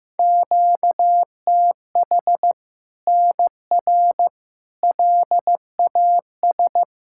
Morse code, heard as a keyed tone, about 15 wpm, 700 Hz. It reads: QTH NR LAS
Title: Practice QSO - Exchange 1 - Beginning - 15wpm